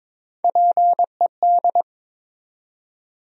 Morse code, heard as a keyed tone, about 22 wpm, 700 Hz. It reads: PED
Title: USA Call Sign Postfixes - 3 Letters 22wpm